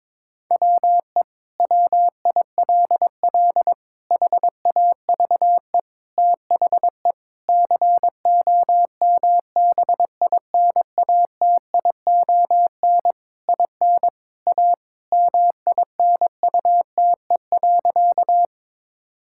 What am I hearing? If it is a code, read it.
Morse code, heard as a keyed tone, about 22 wpm, 700 Hz. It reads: WE WILL HAVE THE COMBINATION IN A MINUTE.
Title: Sentences from Top 1000 Words - Review Part 6 - 22wpm